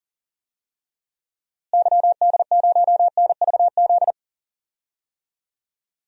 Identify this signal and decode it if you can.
Morse code, heard as a keyed tone, about 39 wpm, 700 Hz. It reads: YD0NVZ
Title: DXCC-100 Call Signs - 6 Characters - Repeated 3x - 40wpm